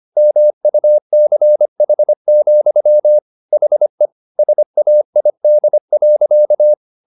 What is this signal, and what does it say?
Morse code, heard as a keyed tone, about 25 wpm, 600 Hz. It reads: MUCH, HE SAID.
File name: the old man and the sea part 1 25wpm